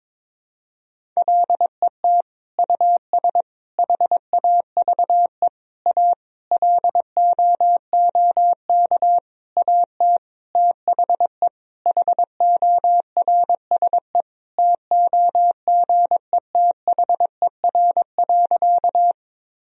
Morse code, heard as a keyed tone, about 22 wpm, 700 Hz. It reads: LET US HAVE A LOOK AT THE HORSE TOGETHER.
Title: Sentences from Top 1000 Words - Review Part 1 - 22wpm